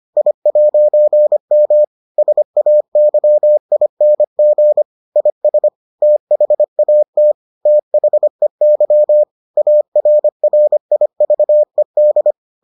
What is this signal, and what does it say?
Morse code, heard as a keyed tone, about 25 wpm, 600 Hz. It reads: I'M SAYING IS THAT THEY ARRIVED